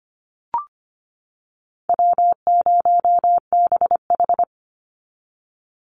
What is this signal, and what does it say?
Morse code, heard as a keyed tone, about 25 wpm, 700 Hz. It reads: W0BH
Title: Easy Call Signs - 25wpm